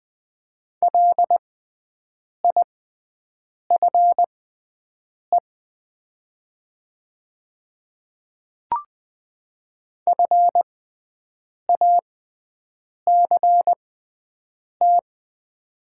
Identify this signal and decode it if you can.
Morse code, heard as a keyed tone, about 20 wpm, spaced as for 7 wpm, 700 Hz. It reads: LIFE FACT
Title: 6x Character Spacing - Top 500 Words - 20wpm